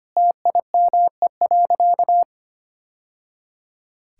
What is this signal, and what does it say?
Morse code, heard as a keyed tone, about 25 wpm, 700 Hz. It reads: TIME.
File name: practice / 25wpm / Sentences with 2-Word Phrases - 25wpm